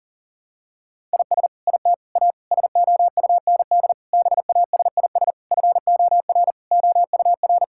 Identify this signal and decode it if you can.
Morse code, heard as a keyed tone, about 40 wpm, 700 Hz. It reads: IS IT A SOUND BASIS FOR OUR
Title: Sentences from Top 1000 Words - Review Part 5 - 40wpm